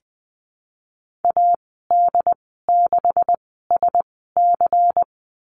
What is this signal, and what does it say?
Morse code, heard as a keyed tone, about 20 wpm, 700 Hz. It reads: A D 6 S C
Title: 154 Lesson 25g - Letter p - Call Signs 20wpm